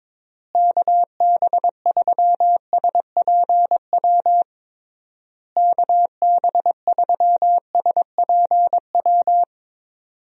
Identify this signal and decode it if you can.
Morse code, heard as a keyed tone, about 22 wpm, 700 Hz. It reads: KB3SPW KB3SPW
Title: USA Call Signs - 6 Characters - Repeated 3x - 22wpm